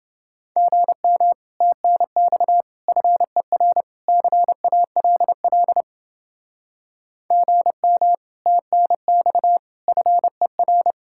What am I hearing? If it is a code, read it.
Morse code, heard as a keyed tone, about 29 wpm, 700 Hz. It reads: GM TNX FER CALL GM TNX FER